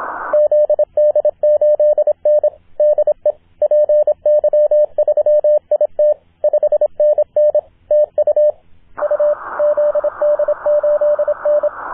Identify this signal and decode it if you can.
Morse code, heard as a keyed tone, about 26 wpm, 600 Hz, with radio noise. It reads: ZD8N DE PY3IT 5NN TU U ZD8N
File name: zd8n-10m-band-cw